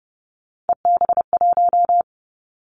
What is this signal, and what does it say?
Morse code, heard as a keyed tone, about 30 wpm, 700 Hz. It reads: EB1
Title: Three Challenging Letter-Number Combinations 30wpm